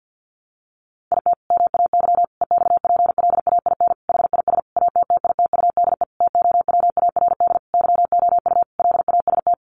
Morse code, heard as a keyed tone, about 50 wpm, 700 Hz. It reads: IT MAY EXPLAIN HIS ATTITUDE TOWARD YOU LAST
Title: Sentences from Top 1300 Words 50wpm